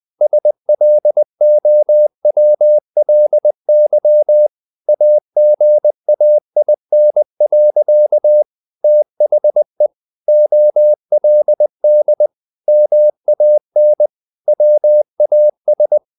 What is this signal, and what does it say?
Morse code, heard as a keyed tone, about 20 wpm, 600 Hz. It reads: SLOWLY AGAIN. THE OLD MAN WAS
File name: the old man and the sea part 2 20wpm